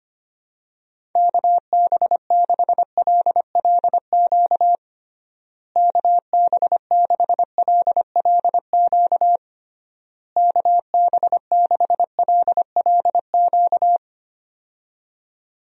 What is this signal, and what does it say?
Morse code, heard as a keyed tone, about 25 wpm, 700 Hz. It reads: KB6LLQ KB6LLQ KB6LLQ
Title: USA Call Signs - 6 Characters - Repeated 3x - 25wpm